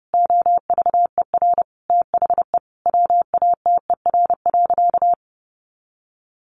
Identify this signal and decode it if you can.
Morse code, heard as a keyed tone, about 30 wpm, 700 Hz. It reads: OVER THE WATER.
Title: Sentences from Top 300 Words 30wpm